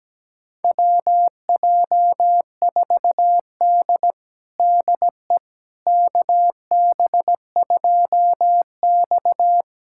Morse code, heard as a keyed tone, about 17 wpm, 700 Hz. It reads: WJ4D DE KB2X
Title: USA Callsign DE Callsign - 4 Characters - 17wpm